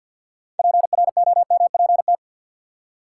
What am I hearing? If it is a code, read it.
Morse code, heard as a keyed tone, about 47 wpm, 700 Hz. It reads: PROMPT
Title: Six Letter Words - 50wpm